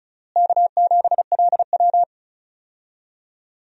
Morse code, heard as a keyed tone, about 34 wpm, 700 Hz. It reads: K7LW
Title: USA Call Signs - 4 Characters - Repeated 3x - 35wpm